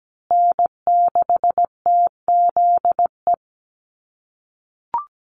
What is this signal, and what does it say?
Morse code, heard as a keyed tone, about 17 wpm, 700 Hz. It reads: N6TZE
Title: USA Call Signs Repeated 2x 17wpm